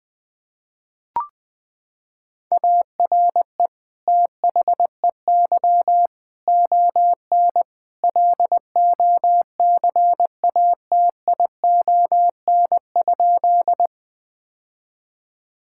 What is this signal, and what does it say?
Morse code, heard as a keyed tone, about 20 wpm, 700 Hz. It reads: ARE THEY ON LOCATION?
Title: Sentences from Top 1000 Words - Review Part 6 - 20wpm